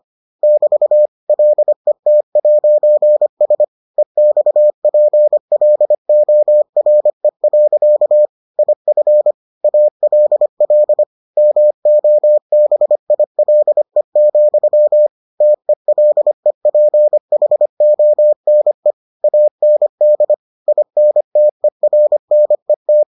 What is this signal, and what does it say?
Morse code, heard as a keyed tone, about 25 wpm, 600 Hz. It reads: <BT> LET'S EXPLORE. IF ALL MOBILE, TELEPHONE AND INTERNET